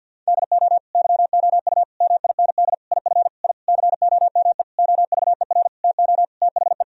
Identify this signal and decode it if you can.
Morse code, heard as a keyed tone, about 50 wpm, 700 Hz. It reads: DO YOU MIND IF I COME OVER TO THE
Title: Sentences from Top 1000 Words - Review Part 2 - 50wpm